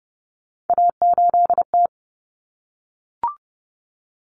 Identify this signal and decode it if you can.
Morse code, heard as a keyed tone, about 30 wpm, 700 Hz. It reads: A8T
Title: Common Cut Numbers - 3-Digit 30wpm